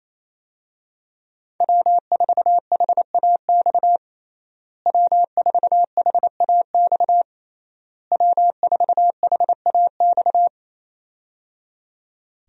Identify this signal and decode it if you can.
Morse code, heard as a keyed tone, about 28 wpm, 700 Hz. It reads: W4HAX W4HAX W4HAX
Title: USA Call Signs - 5 Characters - Repeated 3x - 28wpm